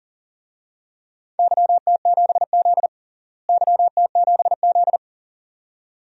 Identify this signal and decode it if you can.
Morse code, heard as a keyed tone, about 37 wpm, 700 Hz. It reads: YT7Z YT7Z
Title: DXCC-100 Call Signs - 4 Characters - Repeated 2x - 40wpm